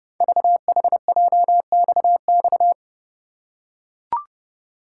Morse code, heard as a keyed tone, about 30 wpm, 700 Hz. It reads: VHJXX
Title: Code Groups - 5 Letters - 30wpm